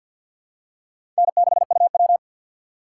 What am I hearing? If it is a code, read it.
Morse code, heard as a keyed tone, about 54 wpm, 700 Hz. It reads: N6UW
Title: USA Call Signs - 6x Character Spacing - 50wpm